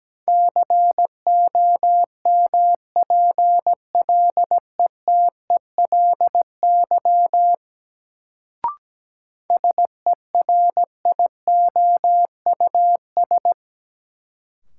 Morse code, heard as a keyed tone, about 17 wpm, 700 Hz. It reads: COMPLETELY SERIOUS
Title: Sentences from Top 1100 Words - New Word Drill - 17wpm